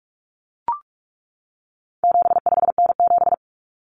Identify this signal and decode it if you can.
Morse code, heard as a keyed tone, about 44 wpm, 700 Hz. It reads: 75N7
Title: Common Cut Numbers - 4-Digit 45wpm